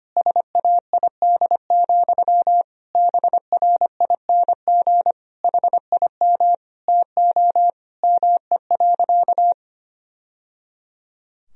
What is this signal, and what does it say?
Morse code, heard as a keyed tone, about 25 wpm, 700 Hz. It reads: SAID, BRING HIM TO ME.
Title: Sentences from Top 1000 Words - Review Part 3 - 25wpm